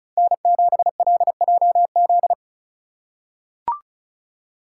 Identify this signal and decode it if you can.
Morse code, heard as a keyed tone, about 36 wpm, 700 Hz. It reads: N7LJZ
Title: USA Call Signs - 5 Characters - 2x Character Spacing - 35wpm